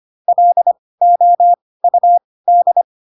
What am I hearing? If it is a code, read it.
Morse code, heard as a keyed tone, about 25 wpm, spaced as for 18 wpm, 700 Hz. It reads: LOUD
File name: Pinocchio / Pinocchio ch2 12 25 ik1hge